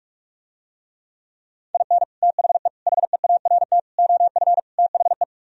Morse code, heard as a keyed tone, about 46 wpm, 700 Hz. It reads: IN THE HEART OF THE